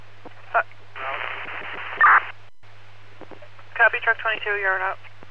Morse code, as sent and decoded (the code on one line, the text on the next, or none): none